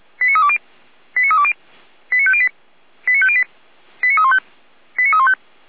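Five-tone selective call (ZVEI), heard as quick runs of five tones, tones of about 70 ms. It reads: 89320 89320 89508 89508 89315 89315
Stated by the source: SDR-received ZVEI five-tone call 89305, Feuerwehr Orlamünde.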